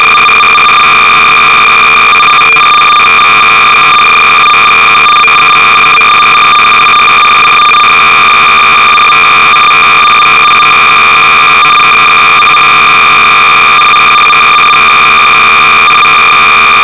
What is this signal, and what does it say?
Cassette data (Kansas City type).
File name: BASIC-Sorcerer-300baud